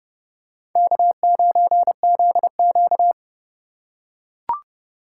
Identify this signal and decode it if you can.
Morse code, heard as a keyed tone, about 30 wpm, 700 Hz. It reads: K9ZQ
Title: USA Call Signs - 3x Character Spacing - 30wpm